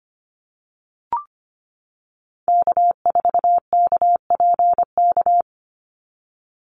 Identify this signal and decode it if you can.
Morse code, heard as a keyed tone, about 25 wpm, 690 Hz. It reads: K4KPK
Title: NA SOTA Activator Callsigns 25wpm